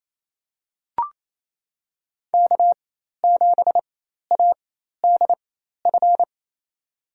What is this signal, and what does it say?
Morse code, heard as a keyed tone, about 28 wpm, spaced as for 13 wpm, 690 Hz. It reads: K7ADF
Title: USA Call Signs - 5 Characters - 4x Character Spacing - 28wpm